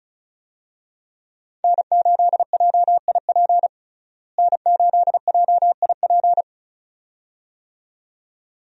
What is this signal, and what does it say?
Morse code, heard as a keyed tone, about 36 wpm, 700 Hz. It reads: N8JIP N8JIP
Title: USA Call Signs - 5 Characters - Repeated 2x - 35wpm